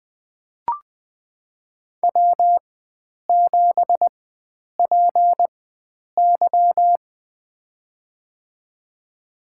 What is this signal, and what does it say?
Morse code, heard as a keyed tone, about 20 wpm, spaced as for 9 wpm, 700 Hz. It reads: W7PY
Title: USA Call Signs - 4 Characters - 4x Character Spacing - 20wpm